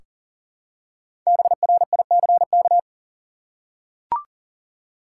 Morse code, heard as a keyed tone, about 40 wpm, 700 Hz. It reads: BRICK
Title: Five Letter Words - 40wpm